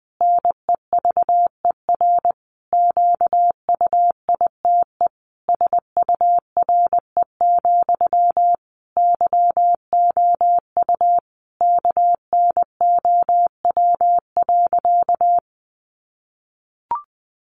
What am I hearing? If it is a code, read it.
Morse code, heard as a keyed tone, about 20 wpm, 700 Hz. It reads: NEVER QUITE SURE, YOU KNOW.
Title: Sentences from Top 700 Words 20wpm